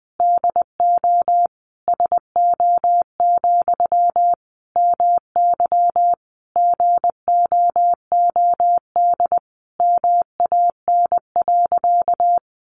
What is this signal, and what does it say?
Morse code, heard as a keyed tone, about 20 wpm, 700 Hz. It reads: DO SO, MY GOOD MAN.
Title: Sentences from Top 1200 Words 20wpm